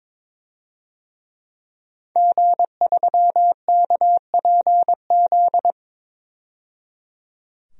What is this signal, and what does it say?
Morse code, heard as a keyed tone, about 22 wpm, 700 Hz. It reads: G3KPZ
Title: DXCC-100 Call Signs - 5 Characters - 22wpm